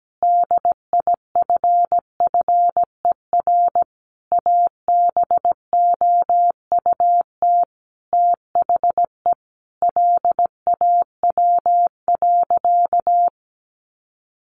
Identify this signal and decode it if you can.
Morse code, heard as a keyed tone, about 17 wpm, 700 Hz. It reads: DIFFER ABOUT THE LAW.
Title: Sentences from Top 1400 Words 17wpm